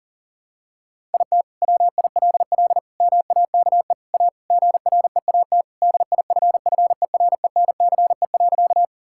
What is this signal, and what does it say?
Morse code, heard as a keyed tone, about 40 wpm, 700 Hz. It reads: IT WILL MAKE A GREAT DIFFERENCE.
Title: Sentences from Top 1000 Words - Review Part 5 - 40wpm